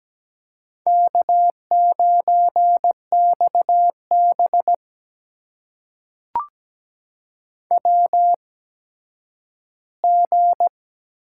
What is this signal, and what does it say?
Morse code, heard as a keyed tone, about 17 wpm, spaced as for 10 wpm, 700 Hz. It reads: K9XB WG
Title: USA Call Signs - 4 Characters - 8x Character Spacing - 17wpm